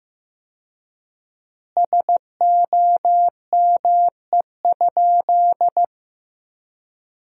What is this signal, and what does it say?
Morse code, heard as a keyed tone, about 15 wpm, 700 Hz. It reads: SOME?
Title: 4x Word-Spacing - Sentences from Top 100 Words - 15wpm